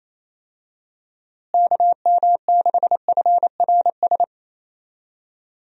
Morse code, heard as a keyed tone, about 27 wpm, 700 Hz. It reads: KM6FRS
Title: USA Call Signs - 6 Characters - Repeated 3x - 28wpm